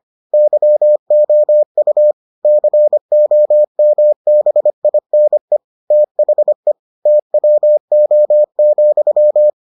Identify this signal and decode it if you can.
Morse code, heard as a keyed tone, about 25 wpm, 600 Hz. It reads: YOU COMBINE THE TWO,